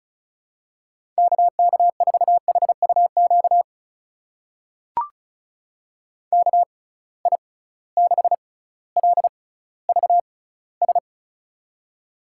Morse code, heard as a keyed tone, about 34 wpm, spaced as for 18 wpm, 700 Hz. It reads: KK4HUQ KI6LVS
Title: USA Call Signs - 6 Characters - 6x Character Spacing - 35wpm